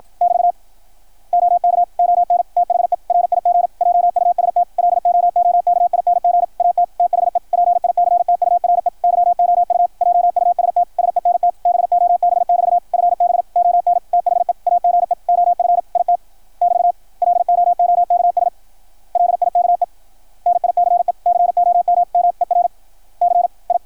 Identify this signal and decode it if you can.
Morse code, heard as a keyed tone, about 53 wpm, 690 Hz, with radio noise. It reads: <BT> OK ON THE RIG JUST LOOKING AT THE PICTURE YOU JUST SENT BOB<BT> FB ON THE AGE OF IT <BT> LOOKS LIKE NICE COMMERCI
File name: archived / CRSnet